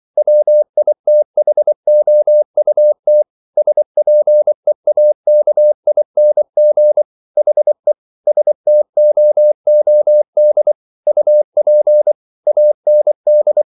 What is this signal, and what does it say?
Morse code, heard as a keyed tone, about 24 wpm, 600 Hz. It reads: WITHOUT SPEAKING HE STOOD UP AND